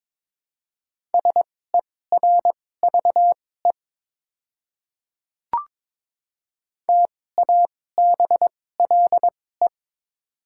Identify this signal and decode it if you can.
Morse code, heard as a keyed tone, about 22 wpm, spaced as for 16 wpm, 700 Hz. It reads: SERVE TABLE
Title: 2x Character Spacing - Top 500 Words - 22wpm